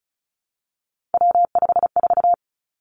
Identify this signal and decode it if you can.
Morse code, heard as a keyed tone, about 36 wpm, 700 Hz. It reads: W54
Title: Three Challenging Letter-Number Combinations 35wpm